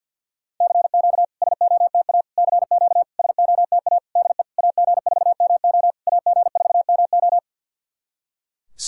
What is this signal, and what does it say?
Morse code, heard as a keyed tone, about 50 wpm, 690 Hz. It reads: CQ SOTA CQ SOTA DE AG4MO AG4MO
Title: SOTA QSO Practice 50wpm